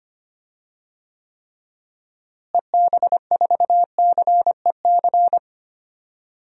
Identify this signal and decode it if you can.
Morse code, heard as a keyed tone, about 25 wpm, 700 Hz. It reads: EB4CEC
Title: DXCC-100 Call Signs - 6 Characters - Repeated 3x - 25wpm